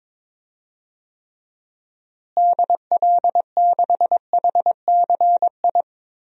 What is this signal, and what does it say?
Morse code, heard as a keyed tone, about 22 wpm, 700 Hz. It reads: DL6HCI
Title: DXCC-100 Call Signs - 6 Characters - Repeated 3x - 22wpm